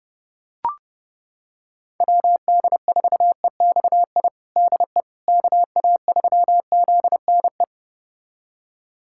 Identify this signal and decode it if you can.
Morse code, heard as a keyed tone, about 28 wpm, 700 Hz. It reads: WD4EXI DE KA3ZNE
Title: USA Callsign DE Callsign - 6 Characters - 30wpm